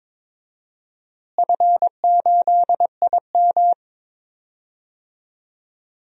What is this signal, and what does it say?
Morse code, heard as a keyed tone, about 22 wpm, 700 Hz. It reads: F8IM
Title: DXCC-100 Call Signs - 4 Characters - Repeated 3x - 22wpm